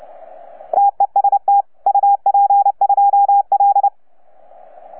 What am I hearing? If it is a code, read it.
Morse code, heard as a keyed tone, about 29 wpm, 810 Hz. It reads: TEST UP2L
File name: UP2L-20-CW